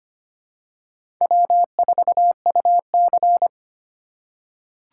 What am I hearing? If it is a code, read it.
Morse code, heard as a keyed tone, about 25 wpm, 700 Hz. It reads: W4UC
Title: USA Call Signs - 4 Characters - 2x Character Spacing - 25wpm